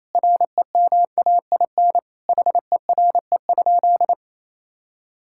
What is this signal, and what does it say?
Morse code, heard as a keyed tone, about 28 wpm, 700 Hz. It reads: REMAIN HERE?
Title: Sentences from Top 1600 Words 28wpm